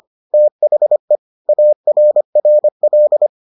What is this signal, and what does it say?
Morse code, heard as a keyed tone, about 25 wpm, 600 Hz. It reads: THE ARRL